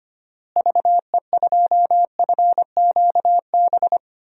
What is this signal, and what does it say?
Morse code, heard as a keyed tone, about 25 wpm, 700 Hz. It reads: VE2FQB
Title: DXCC-100 Call Signs - 6 Characters - Repeated 2x - 25wpm